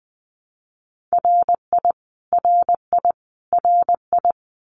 Morse code, heard as a keyed tone, about 20 wpm, 700 Hz. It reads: RI RI RI
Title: US States Repeated 3x 20wpm